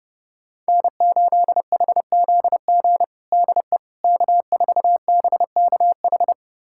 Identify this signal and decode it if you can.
Morse code, heard as a keyed tone, about 32 wpm, 700 Hz. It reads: N8HZG DE K4BKH
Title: USA Callsign DE Callsign 30wpm